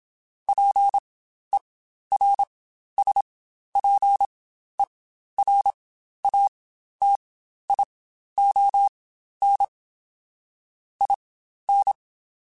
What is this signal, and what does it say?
Morse code, heard as a keyed tone, about 27 wpm, spaced as for 12 wpm, 800 Hz. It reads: PERSPERATION IN
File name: Quotes Spaced / Quotes Spaced-mp3s / Persperation sp @25